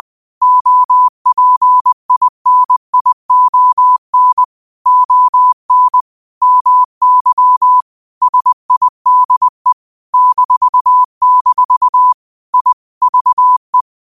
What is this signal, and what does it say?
Morse code, heard as a keyed tone, about 20 wpm, 1000 Hz. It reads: OPINION ON MY SIDE -- I VE
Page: New quotes every day in morse code at 20 Words per minute.